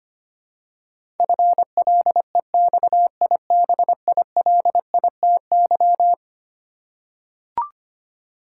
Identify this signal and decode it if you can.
Morse code, heard as a keyed tone, about 25 wpm, 700 Hz. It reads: FLEXIBILITY
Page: morse-code-ninja-practice-25wpm